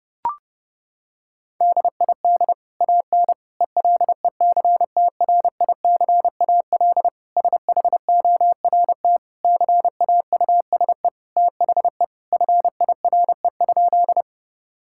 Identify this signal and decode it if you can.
Morse code, heard as a keyed tone, about 32 wpm, 700 Hz. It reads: DID AN ELECTRICAL SHORT CAUSE THE FIRE?
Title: Sentences from Top 1000 Words - Review Part 6 - 30wpm